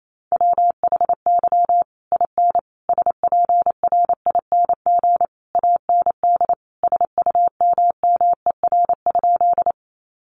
Morse code, heard as a keyed tone, about 28 wpm, 700 Hz. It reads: WHY IN SPRING AND SUMMER?
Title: Sentences from Top 1200 Words 28wpm